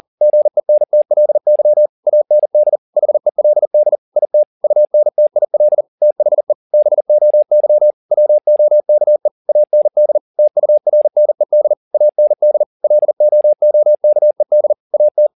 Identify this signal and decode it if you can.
Morse code, heard as a keyed tone, about 40 wpm, 600 Hz. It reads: GENTLY AND HELD IT UNTIL THE BOY WOKE AND TURNED AND LOOKED AT